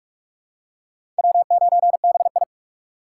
Morse code, heard as a keyed tone, about 41 wpm, 700 Hz. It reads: W9BI
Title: USA Call Signs - 4x Character Spacing - 45wpm